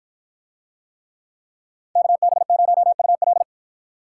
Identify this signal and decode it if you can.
Morse code, heard as a keyed tone, about 53 wpm, 700 Hz. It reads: KB0UL